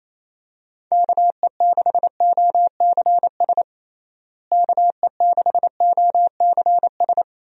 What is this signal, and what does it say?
Morse code, heard as a keyed tone, about 28 wpm, 700 Hz. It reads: KE6OCS KE6OCS
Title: USA Call Signs - 6 Characters - Repeated 3x - 28wpm